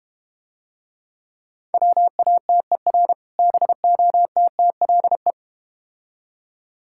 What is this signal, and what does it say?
Morse code, heard as a keyed tone, about 31 wpm, 700 Hz. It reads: WATER BOTTLE
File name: practice / 25wpm / Difficult Words - Speed-Racing with Hints - 25wpm